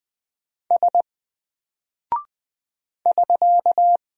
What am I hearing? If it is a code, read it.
Morse code, heard as a keyed tone, about 20 wpm, 700 Hz. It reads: S <SK>
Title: Single Character (Letter:Number:Punctuation:Prosign) 20wpm